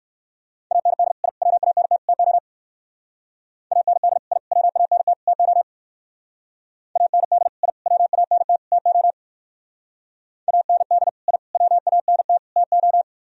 Morse code, heard as a keyed tone, about 61 wpm, 700 Hz. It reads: AND I WANT TO AND I WANT TO AND I WANT TO AND I WANT TO